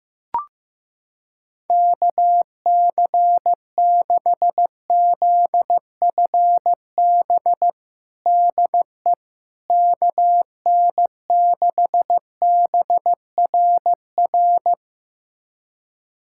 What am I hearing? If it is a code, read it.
Morse code, heard as a keyed tone, about 15 wpm, 700 Hz. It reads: KC6ZFB DE KN6BRR
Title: USA Callsign DE Callsign 15wpm